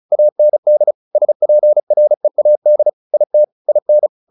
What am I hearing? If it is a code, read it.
Morse code, heard as a keyed tone, about 35 wpm, 600 Hz. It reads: AND SPREAD IT IN
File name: the old man and the sea / the old man and the sea 1 35wpm